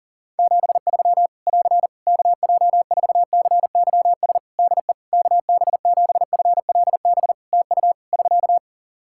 Morse code, heard as a keyed tone, about 40 wpm, 700 Hz. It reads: 73 <AR> KJ4CYS DE KB7FLB TU <SK>
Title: Practice QSO - Exchange 4 - Ending - 40wpm